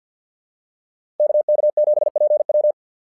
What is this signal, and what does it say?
Morse code, heard as a keyed tone, about 49 wpm, 600 Hz. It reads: KK7PW
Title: SOTA QSO Practice 50wpm